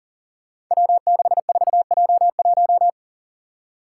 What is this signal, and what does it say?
Morse code, heard as a keyed tone, about 42 wpm, 700 Hz. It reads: W64J1
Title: Five Challenging Letter-Number Combinations 40wpm